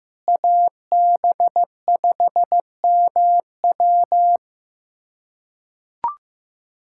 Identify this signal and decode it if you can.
Morse code, heard as a keyed tone, about 16 wpm, 700 Hz. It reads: AB5MW
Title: USA Callsign DE Callsign - 5 Characters - 15wpm